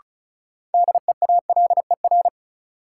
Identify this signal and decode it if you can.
Morse code, heard as a keyed tone, about 36 wpm, 700 Hz. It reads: DEALER
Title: Sentences from Top 1100 Words - New Word Drill - 35wpm